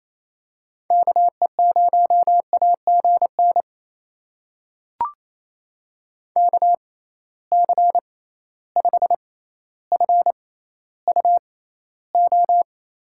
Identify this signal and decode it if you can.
Morse code, heard as a keyed tone, about 28 wpm, spaced as for 14 wpm, 700 Hz. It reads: KE0AGN KC5FUO
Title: USA Call Signs - 6 Characters - 6x Character Spacing - 28wpm